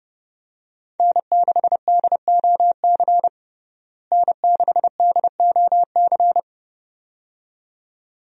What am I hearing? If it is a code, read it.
Morse code, heard as a keyed tone, about 30 wpm, 700 Hz. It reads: N6DOC N6DOC
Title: USA Call Signs - 5 Characters - Repeated 2x - 30wpm